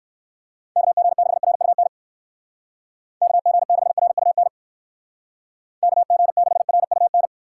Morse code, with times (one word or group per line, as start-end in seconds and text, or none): 0.8-1.9 KG6RUN
3.2-4.5 KG6RUN
5.8-7.3 KG6RUN